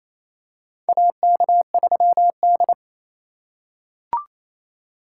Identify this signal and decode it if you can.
Morse code, heard as a keyed tone, about 27 wpm, 700 Hz. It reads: AK3D
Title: USA Call Signs - 4 Characters - 3x Character Spacing - 28wpm